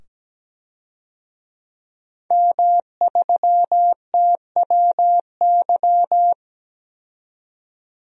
Morse code, heard as a keyed tone, about 17 wpm, 700 Hz. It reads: M3TWY
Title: DXCC-100 Call Signs - 5 Characters - 17wpm